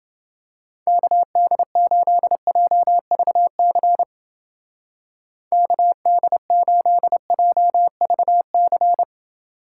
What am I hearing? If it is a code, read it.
Morse code, heard as a keyed tone, about 29 wpm, 700 Hz. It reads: KD8JVC KD8JVC